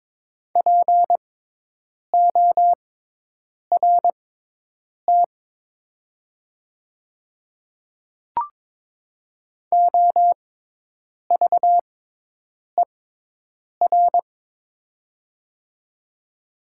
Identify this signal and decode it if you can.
Morse code, heard as a keyed tone, about 22 wpm, spaced as for 8 wpm, 700 Hz. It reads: PORT OVER
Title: 6x Character Spacing - Top 200 Words - 22wpm